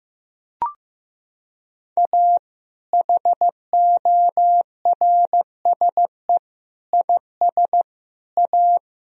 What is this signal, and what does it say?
Morse code, heard as a keyed tone, about 15 wpm, 700 Hz. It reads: A HORSE IS A
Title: Sentences from Top 1000 Words - Review Part 1 - 15wpm